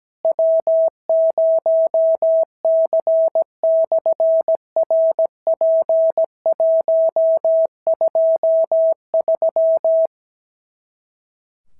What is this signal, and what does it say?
Morse code, heard as a keyed tone, about 17 wpm, 650 Hz. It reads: W0C/RP123
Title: SOTA QSO Practice 17wpm